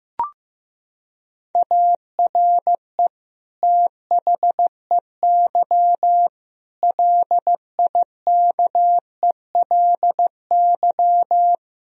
Morse code, heard as a keyed tone, about 15 wpm, 700 Hz. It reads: ARE THEY LIKELY
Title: Sentences from Top 1000 Words - Review Part 4 - 15wpm